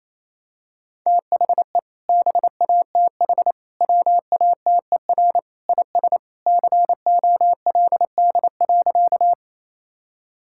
Morse code, heard as a keyed tone, about 28 wpm, 700 Hz. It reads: THE BATH WATER IS COLD.
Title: Sentences from Top 1600 Words 28wpm